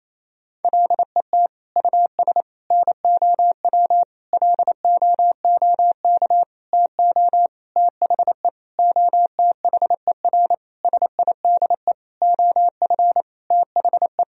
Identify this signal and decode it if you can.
Morse code, heard as a keyed tone, about 28 wpm, 700 Hz. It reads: LET US NOW LOOK TO THE OTHER SIDE OF THE
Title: Sentences from Top 1000 Words - Review Part 1 - 28wpm